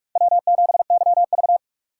Morse code, heard as a keyed tone, about 46 wpm, 700 Hz. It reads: W7YV
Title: USA Call Signs - 4 Characters - 4x Character Spacing - 45wpm